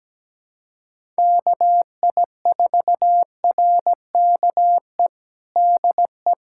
Morse code, heard as a keyed tone, about 17 wpm, 700 Hz. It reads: KI4RKE DE
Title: USA Callsign DE Callsign - 6 Characters - 17wpm